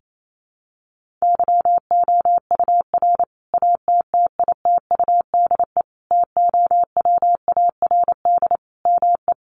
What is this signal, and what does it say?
Morse code, heard as a keyed tone, about 28 wpm, 700 Hz. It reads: YOUR ATTITUDE TOWARD ME
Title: Sentences from Top 1300 Words 28wpm